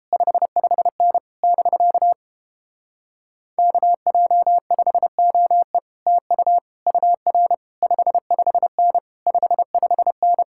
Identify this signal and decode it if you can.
Morse code, heard as a keyed tone, about 31 wpm, 700 Hz. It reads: 55N <BK> KJ5OE TU UR 55N 55N